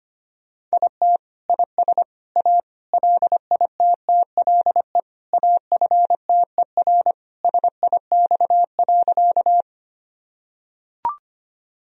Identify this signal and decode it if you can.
Morse code, heard as a keyed tone, about 25 wpm, 700 Hz. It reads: IT IS A LITTLE AFTER SIX.
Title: Sentences from Top 1000 Words - Review Part 2 - 25wpm